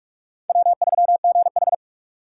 Morse code, heard as a keyed tone, about 47 wpm, 700 Hz. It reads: W3GH
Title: USA Call Signs - 4 Characters - 3x Character Spacing - 45wpm